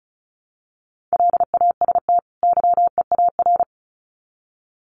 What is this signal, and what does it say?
Morse code, heard as a keyed tone, about 34 wpm, 700 Hz. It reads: LAST YEAR
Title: Sets of 2 Words 35wpm